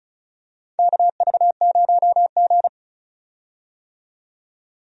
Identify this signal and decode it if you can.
Morse code, heard as a keyed tone, about 36 wpm, 700 Hz. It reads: KV0G
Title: USA Call Signs - 4 Characters - 35wpm